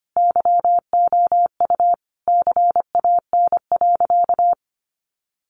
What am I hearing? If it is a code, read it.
Morse code, heard as a keyed tone, about 25 wpm, 700 Hz. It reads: YOU CAN.
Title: Sentences from Top 1400 Words 25wpm